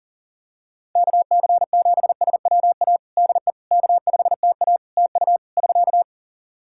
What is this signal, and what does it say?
Morse code, heard as a keyed tone, about 40 wpm, 700 Hz. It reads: KC7SWA DE K5TA TU <SK>
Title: Practice QSO - Exchange 4 - Ending - 40wpm